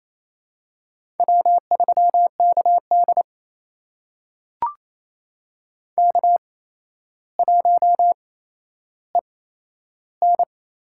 Morse code, heard as a keyed tone, about 28 wpm, spaced as for 12 wpm, 700 Hz. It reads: W3KD K1EN
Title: USA Call Signs - 4 Characters - 8x Character Spacing - 28wpm